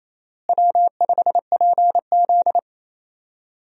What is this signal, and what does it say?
Morse code, heard as a keyed tone, about 27 wpm, 700 Hz. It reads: W5PZ
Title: USA Call Signs - 4 Characters - Repeated 3x - 28wpm